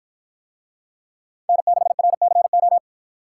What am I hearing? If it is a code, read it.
Morse code, heard as a keyed tone, about 53 wpm, 700 Hz. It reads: N6RCO